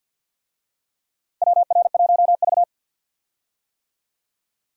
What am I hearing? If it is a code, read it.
Morse code, heard as a keyed tone, about 52 wpm, 700 Hz. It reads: WR1V
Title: USA Call Signs - 4 Characters - Repeated 2x - 50wpm